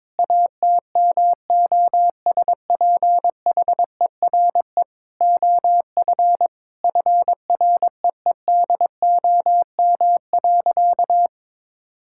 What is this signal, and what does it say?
Morse code, heard as a keyed tone, about 22 wpm, 700 Hz. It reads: ATMOSPHERE OF FREEDOM.
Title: Sentences from Top 1600 Words 22wpm